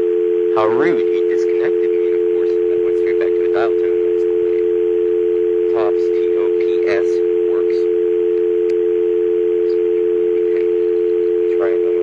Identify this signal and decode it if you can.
Dial tone (North America), 350 + 440 Hz, with speech.
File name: TurtleCreekEWSD